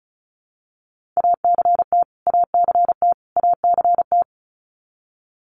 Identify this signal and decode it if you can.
Morse code, heard as a keyed tone, about 34 wpm, 700 Hz. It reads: ACT ACT ACT
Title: Top 500 Words Repeated 35wpm